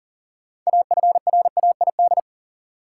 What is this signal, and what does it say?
Morse code, heard as a keyed tone, about 38 wpm, 700 Hz. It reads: AFRAID
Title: Sentences from Top 1500 Words - New Word Drill Alt - 40wpm